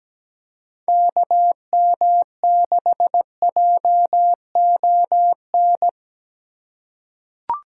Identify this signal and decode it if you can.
Morse code, heard as a keyed tone, about 17 wpm, 700 Hz. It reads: KM6JON
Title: USA Call Signs - 6 Characters - 3x Character Spacing - 17wpm